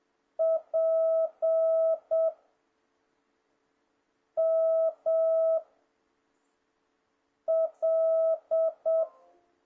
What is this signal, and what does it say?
Morse code, heard as a keyed tone, about 7 wpm, 650 Hz. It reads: P M L
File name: jh8cbh / 2026cw